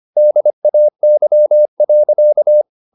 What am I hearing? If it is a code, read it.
Morse code, heard as a keyed tone, about 25 wpm, 600 Hz. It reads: DAY.
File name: the old man and the sea part 1 25wpm